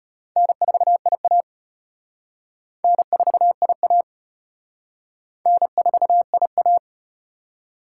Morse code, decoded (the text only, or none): N4IA N4IA N4IA